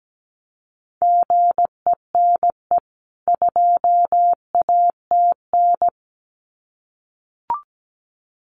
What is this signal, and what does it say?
Morse code, heard as a keyed tone, about 17 wpm, 700 Hz. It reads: GENE 2ATN
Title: CWT Exchange 17wpm